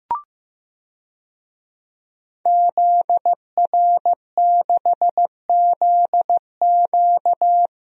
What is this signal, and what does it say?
Morse code, heard as a keyed tone, about 15 wpm, 700 Hz. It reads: ZR6ZQ
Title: DXCC-100 Call Signs - 5 Characters - 15wpm